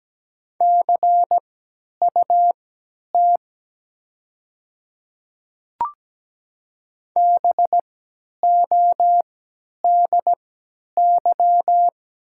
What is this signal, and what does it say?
Morse code, heard as a keyed tone, about 17 wpm, spaced as for 10 wpm, 700 Hz. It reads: CUT BODY
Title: 3x Character Spacing - Top 500 Words - 17wpm